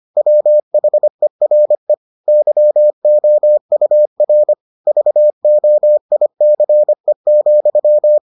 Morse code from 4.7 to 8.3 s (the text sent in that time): VOICE,